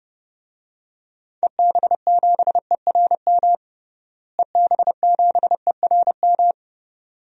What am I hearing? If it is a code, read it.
Morse code, heard as a keyed tone, about 30 wpm, 700 Hz. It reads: EB7ERM EB7ERM
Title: DXCC-100 Call Signs - 6 Characters - Repeated 2x - 30wpm